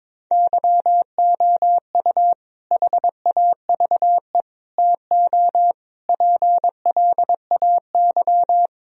Morse code, heard as a keyed tone, about 22 wpm, 700 Hz. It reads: YOU HAVE TO PLAY
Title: Sentences from Top 1000 Words - Review Part 1 - 22wpm